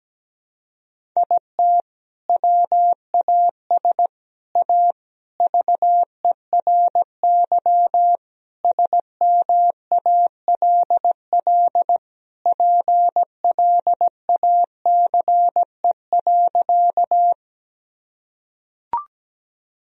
Morse code, heard as a keyed tone, about 17 wpm, 700 Hz. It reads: IT WAS A VERY SMALL PLACE.
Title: Sentences from Top 1000 Words - Review Part 1 - 17wpm